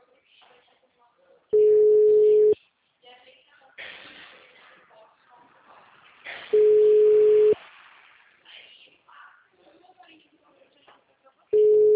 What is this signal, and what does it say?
Ringback tone (Europe, 425 Hz), with speech.
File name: out- -2002- - - .4187